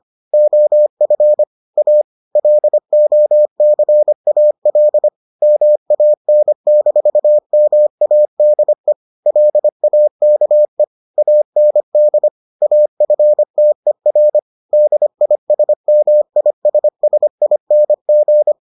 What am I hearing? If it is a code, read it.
Morse code, heard as a keyed tone, about 25 wpm, 600 Hz. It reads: OF A LOCAL MAN-MADE LAKE AND AFTER DISMISSING